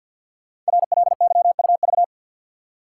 Morse code, heard as a keyed tone, about 51 wpm, 700 Hz. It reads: RLYUV
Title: Code Groups - 5 Letters - 50wpm